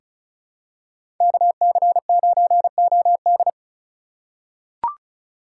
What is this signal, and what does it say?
Morse code, heard as a keyed tone, about 34 wpm, 700 Hz. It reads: KC9OD
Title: USA Call Signs - 4x Character Spacing - 35wpm